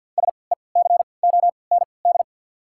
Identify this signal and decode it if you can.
Morse code, heard as a keyed tone, about 51 wpm, spaced as for 28 wpm, 690 Hz. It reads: SECOND
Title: 3x Character Spacing - Top 300 Words - 50wpm